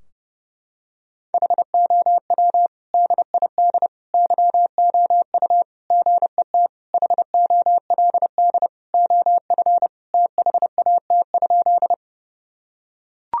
Morse code, heard as a keyed tone, about 30 wpm, 700 Hz. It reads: HOW DID YOU GET HOLD OF THAT?
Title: Sentences from Top 1000 Words - Review Part 2 - 30wpm